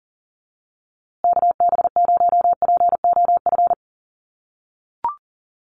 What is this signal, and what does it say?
Morse code, heard as a keyed tone, about 38 wpm, 700 Hz. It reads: KB0POF
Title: USA Call Signs Repeated 2x 40wpm